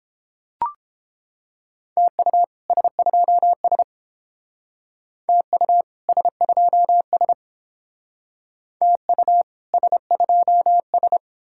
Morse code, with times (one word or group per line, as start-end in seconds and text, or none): 2.0-2.4 TU
2.7-3.8 S2S
5.3-5.8 TU
6.1-7.3 S2S
8.8-9.4 TU
9.7-11.2 S2S